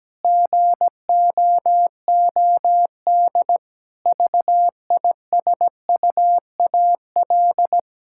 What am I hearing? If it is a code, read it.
Morse code, heard as a keyed tone, about 17 wpm, 700 Hz. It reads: GOOD VISUAL
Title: Sentences from Top 1000 Words - Review Part 6 - 17wpm